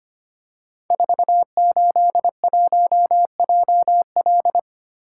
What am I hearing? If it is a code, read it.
Morse code, heard as a keyed tone, about 25 wpm, 700 Hz. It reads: 481JL
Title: Code Groups - 5 Letter:Numbers - 25wpm